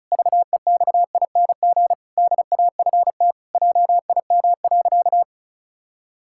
Morse code, heard as a keyed tone, about 34 wpm, 700 Hz. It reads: VEXING DAFT JIM.
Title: Pangram Challenge v2 - 35wpm